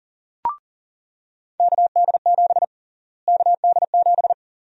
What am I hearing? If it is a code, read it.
Morse code, heard as a keyed tone, about 40 wpm, 700 Hz. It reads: KD7 KD7
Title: USA Call Sign Prefixes - Repeated 2x - 40wpm